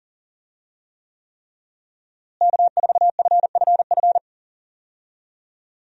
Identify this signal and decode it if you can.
Morse code, heard as a keyed tone, about 40 wpm, 700 Hz. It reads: K4FFF
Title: USA Call Signs - 5 Characters - Repeated 3x - 40wpm